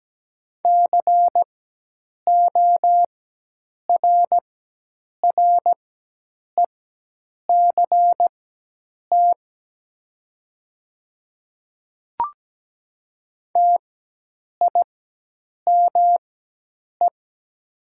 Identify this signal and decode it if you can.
Morse code, heard as a keyed tone, about 17 wpm, spaced as for 8 wpm, 700 Hz. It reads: CORRECT TIME
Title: 4x Character Spacing - Top 500 Words - 17wpm